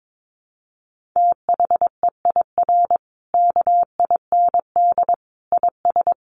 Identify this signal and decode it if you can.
Morse code, heard as a keyed tone, about 22 wpm, 700 Hz. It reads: THEIR KIND IS
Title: Sentences from Top 1500 Words 22wpm